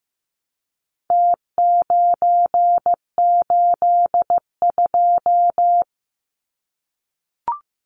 Morse code, heard as a keyed tone, about 15 wpm, 700 Hz. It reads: T982
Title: Numbers - Contest Style - 4-Digit 15wpm